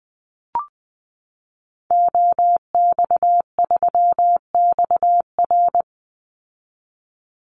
Morse code, heard as a keyed tone, about 20 wpm, 700 Hz. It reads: OX3XR
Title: Call Signs Repeated 3x - 20wpm